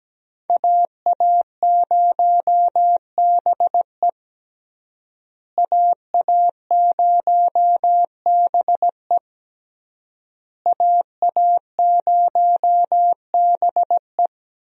Morse code, heard as a keyed tone, about 17 wpm, 700 Hz. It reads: AA0BE AA0BE AA0BE
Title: USA Call Signs - 5 Characters - Repeated 3x - 17wpm